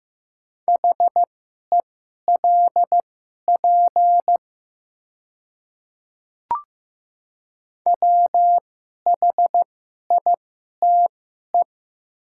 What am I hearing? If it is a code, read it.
Morse code, heard as a keyed tone, about 15 wpm, spaced as for 11 wpm, 690 Hz. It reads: HELP WHITE
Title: 2x Character Spacing - Top 500 Words - 15wpm